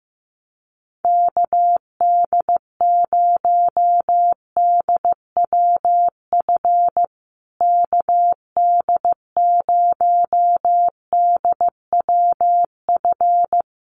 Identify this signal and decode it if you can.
Morse code, heard as a keyed tone, about 15 wpm, 700 Hz. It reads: KD0DWF KD0DWF
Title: USA Call Signs Repeated 2x 15wpm